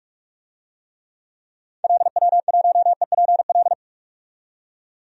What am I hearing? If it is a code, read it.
Morse code, heard as a keyed tone, about 46 wpm, 700 Hz. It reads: LW1EPL
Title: DXCC-100 Call Signs - 6 Characters - 45wpm